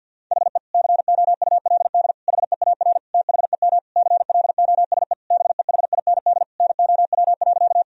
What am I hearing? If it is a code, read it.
Morse code, heard as a keyed tone, about 50 wpm, 700 Hz. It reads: HE COULD HEAR THEM CLOSE BEHIND NOW.
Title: Sentences from Top 1000 Words - Review Part 1 - 50wpm